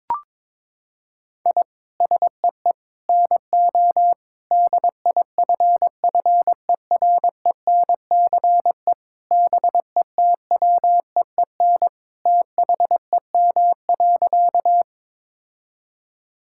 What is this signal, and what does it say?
Morse code, heard as a keyed tone, about 22 wpm, 700 Hz. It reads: I SEE NO DIFFERENCE BETWEEN THEM.
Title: Sentences from Top 1000 Words - Review Part 5 - 22wpm